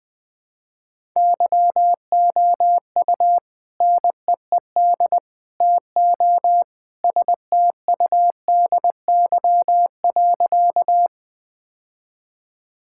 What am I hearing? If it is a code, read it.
Morse code, heard as a keyed tone, about 20 wpm, 700 Hz. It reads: YOU NEED TO STUDY.
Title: Sentences with 3-Word Phrases - 20wpm